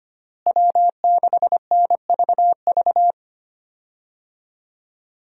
Morse code, heard as a keyed tone, about 25 wpm, 700 Hz. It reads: W6NVV
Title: USA Call Signs - 5 Characters - Repeated 3x - 25wpm